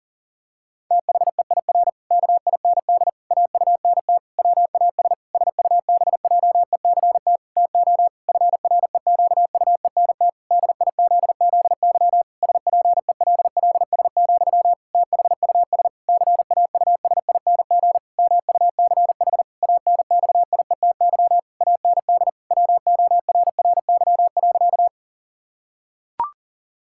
Morse code, heard as a keyed tone, about 40 wpm, 700 Hz. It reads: THEIR KIND AUNT WAS SUBJECT TO FREQUENT DIZZY SPELLS, THUS CAUSING MUCH ANXIETY AND WORRY.
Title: Pangram Challenge v2 - 40wpm